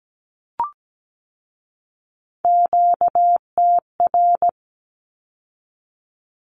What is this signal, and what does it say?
Morse code, heard as a keyed tone, about 17 wpm, 700 Hz. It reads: QTR
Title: Q-Codes with Spoken Meaning - 17wpm